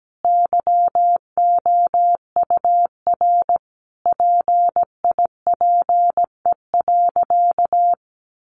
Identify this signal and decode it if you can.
Morse code, heard as a keyed tone, about 17 wpm, 700 Hz. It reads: YOUR PIPE.
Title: Sentences from Top 1500 Words 17wpm